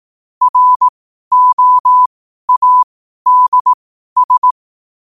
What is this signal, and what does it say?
Morse code, heard as a keyed tone, about 18 wpm, spaced as for 12 wpm, 1000 Hz. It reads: ROADS
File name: qotdfiles / 12wpm